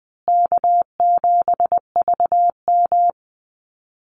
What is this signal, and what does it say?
Morse code, heard as a keyed tone, about 20 wpm, 700 Hz. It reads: K7VM
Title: Easy Call Signs Repeated 2x - 20wpm